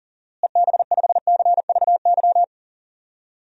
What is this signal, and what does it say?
Morse code, heard as a keyed tone, about 40 wpm, 700 Hz. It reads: EB5CVY
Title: DXCC-100 Call Signs - 6 Characters - Repeated 2x - 40wpm